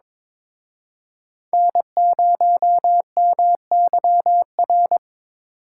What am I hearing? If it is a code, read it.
Morse code, heard as a keyed tone, about 26 wpm, 700 Hz. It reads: N0MYR
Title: Speed Racing - USA 1x3 Call Signs 22wpm